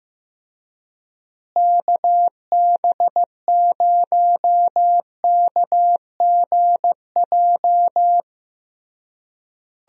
Morse code, heard as a keyed tone, about 15 wpm, 700 Hz. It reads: KB0KGJ
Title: USA Call Signs - 6 Characters - Repeated 3x - 15wpm